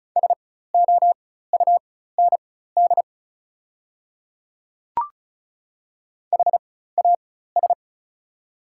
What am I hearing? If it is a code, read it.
Morse code, heard as a keyed tone, about 36 wpm, spaced as for 16 wpm, 690 Hz. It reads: SOUND HAS
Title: 4x Character Spacing - Top 100 Words - 35wpm